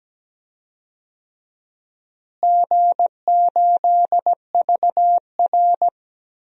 Morse code, heard as a keyed tone, about 17 wpm, 700 Hz. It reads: G8VR
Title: DXCC-100 Call Signs - 4 Characters - Repeated 3x - 17wpm